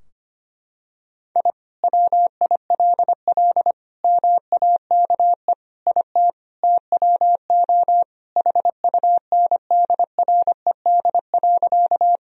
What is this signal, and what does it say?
Morse code, heard as a keyed tone, about 25 wpm, 700 Hz. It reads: I WILL MAKE IT TWO HUNDRED.
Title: Sentences from Top 1000 Words - Review Part 2 - 25wpm